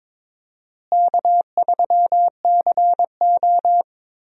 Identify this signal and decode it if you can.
Morse code, heard as a keyed tone, about 22 wpm, 700 Hz. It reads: K3CO
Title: USA Call Signs - 4 Characters - Repeated 2x - 22wpm